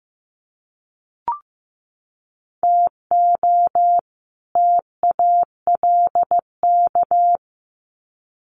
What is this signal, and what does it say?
Morse code, heard as a keyed tone, about 15 wpm, 700 Hz. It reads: TO TALK
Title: Sets of 2 Words 15wpm